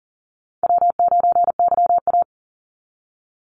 Morse code, heard as a keyed tone, about 40 wpm, 700 Hz. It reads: W9YA
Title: Easy Call Signs - 40wpm